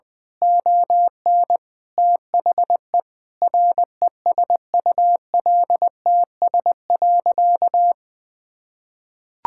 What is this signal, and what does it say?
Morse code, heard as a keyed tone, about 20 wpm, 700 Hz. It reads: ON THE RESULTS.
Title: Sentences with 7-Word Phrases - Spoken Beginning - 20wpm